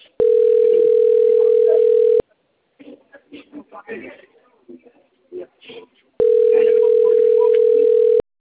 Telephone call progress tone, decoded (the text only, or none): ringback tone